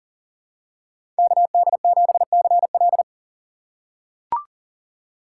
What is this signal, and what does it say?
Morse code, heard as a keyed tone, about 40 wpm, 700 Hz. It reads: KD7CL
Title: USA Call Signs - 5 Characters - 6x Character Spacing - 40wpm